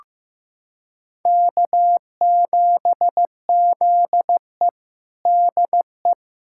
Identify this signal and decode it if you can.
Morse code, heard as a keyed tone, about 15 wpm, 700 Hz. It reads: K7ZE DE
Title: USA Callsign DE Callsign - 4 Characters - 15wpm